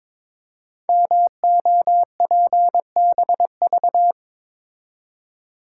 Morse code, heard as a keyed tone, about 22 wpm, 700 Hz. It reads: MOPBV
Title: Code Groups - 5 Letters - 22wpm